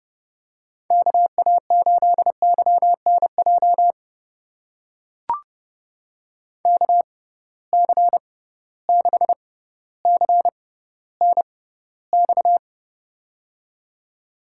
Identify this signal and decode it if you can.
Morse code, heard as a keyed tone, about 30 wpm, spaced as for 15 wpm, 700 Hz. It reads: KA8YNJ KC6CNX
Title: USA Call Signs - 6 Characters - 6x Character Spacing - 30wpm